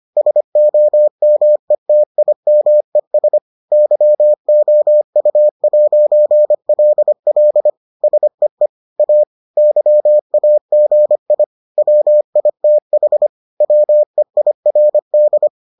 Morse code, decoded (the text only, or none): SOMETIMES YOU'LL SEE A YAGI WITH WEIRD